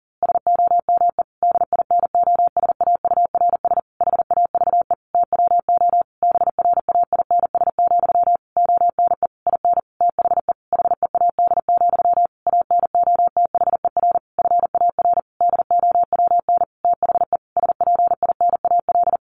Morse code, heard as a keyed tone, about 40 wpm, 700 Hz. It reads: SOME DINOSAURS HAVE TWO BRAINS, ONE IN THE HEAD, ANOTHER FAR DOWN THE SPINAL
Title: A Sound Of Thunder 40wpm